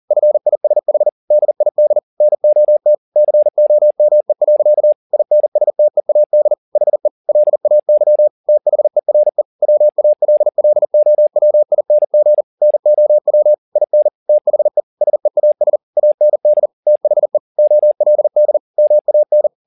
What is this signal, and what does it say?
Morse code, heard as a keyed tone, about 40 wpm, 600 Hz. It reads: FISH DID NOT COME. INSTEAD HE LAY THERE WALLOWING NOW IN THE SEAS AND THE OLD MAN
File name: the old man and the sea 2 40wpm